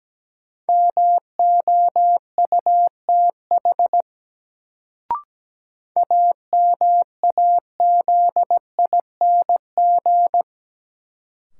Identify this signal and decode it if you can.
Morse code, heard as a keyed tone, about 17 wpm, 700 Hz. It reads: MOUTH AMAZING
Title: Sentences from Top 1500 Words - New Word Drill Alt - 17wpm